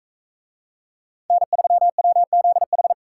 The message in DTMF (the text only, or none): none